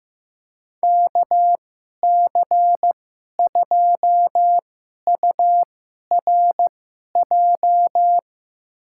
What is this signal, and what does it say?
Morse code, heard as a keyed tone, about 15 wpm, spaced as for 11 wpm, 700 Hz. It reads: KC2URJ
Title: USA Call Signs - 6 Characters - 2x Character Spacing - 15wpm